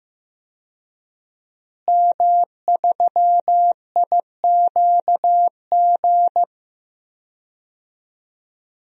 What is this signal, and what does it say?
Morse code, heard as a keyed tone, about 15 wpm, 700 Hz. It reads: M3IQG
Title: DXCC-100 Call Signs - 5 Characters - Repeated 2x - 15wpm